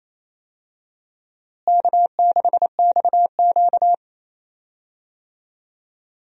Morse code, heard as a keyed tone, about 27 wpm, 700 Hz. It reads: K6XQ
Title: USA Call Signs - 4 Characters - Repeated 3x - 28wpm